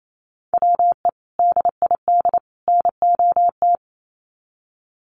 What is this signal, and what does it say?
Morse code, heard as a keyed tone, about 28 wpm, 700 Hz. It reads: WE DID NOT
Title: Sets of 3 Words 28wpm